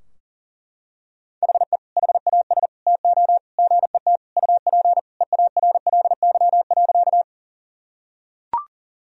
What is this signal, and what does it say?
Morse code, heard as a keyed tone, about 40 wpm, 700 Hz. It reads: HE HAS TO GET UP EARLY.
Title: Sentences from Top 1000 Words - Review Part 2 - 40wpm